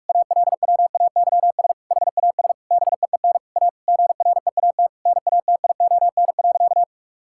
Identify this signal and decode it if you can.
Morse code, heard as a keyed tone, about 46 wpm, 700 Hz. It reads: ALWAYS HAS BEEN A GREAT NATION.
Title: Sentences from Top 1000 Words - Review Part 6 - 45wpm